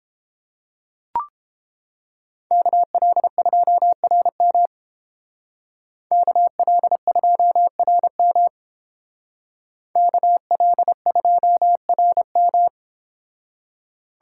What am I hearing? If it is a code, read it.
Morse code, heard as a keyed tone, about 31 wpm, 700 Hz. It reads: KL2RM KL2RM KL2RM